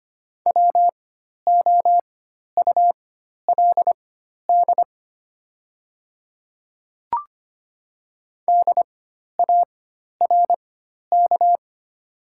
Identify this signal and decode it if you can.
Morse code, heard as a keyed tone, about 25 wpm, spaced as for 12 wpm, 700 Hz. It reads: WOULD DARK
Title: 4x Character Spacing - Top 500 Words - 25wpm